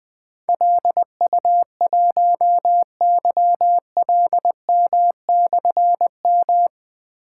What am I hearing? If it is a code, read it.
Morse code, heard as a keyed tone, about 21 wpm, 700 Hz. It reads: LU1YLM/M
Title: Difficult Call Signs 20wpm